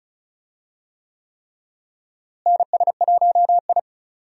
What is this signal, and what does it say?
Morse code, heard as a keyed tone, about 36 wpm, 700 Hz. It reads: NS1I
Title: USA Call Signs - 4 Characters - Repeated 3x - 35wpm